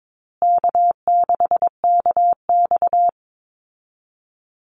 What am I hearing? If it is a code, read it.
Morse code, heard as a keyed tone, about 22 wpm, 700 Hz. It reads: K6KX
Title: Call Signs Repeated 2x - 22wpm